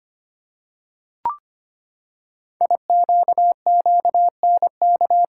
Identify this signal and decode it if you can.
Morse code, heard as a keyed tone, about 25 wpm, 700 Hz. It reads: IQQNK
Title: Code Groups - 5 Letters - 25wpm